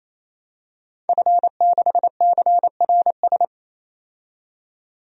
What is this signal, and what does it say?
Morse code, heard as a keyed tone, about 28 wpm, 700 Hz. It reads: F6CRS
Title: DXCC-100 Call Signs - 5 Characters - 28wpm